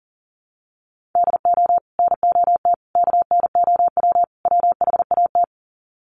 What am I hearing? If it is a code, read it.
Morse code, heard as a keyed tone, about 40 wpm, 700 Hz. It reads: DO NOT KNOW WHAT
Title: Sets of 4 Words 40wpm